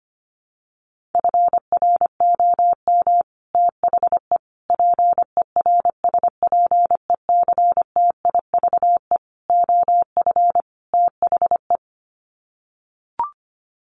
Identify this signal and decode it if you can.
Morse code, heard as a keyed tone, about 25 wpm, 700 Hz. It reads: FROM THE PERSPECTIVE OF THE
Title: Sets of 5 Words Encore2 25wpm